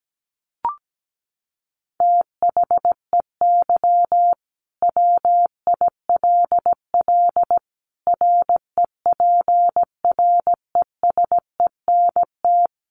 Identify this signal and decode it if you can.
Morse code, heard as a keyed tone, about 17 wpm, 700 Hz. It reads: THEY WILL REPRESENT
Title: Sentences from Top 1200 Words 17wpm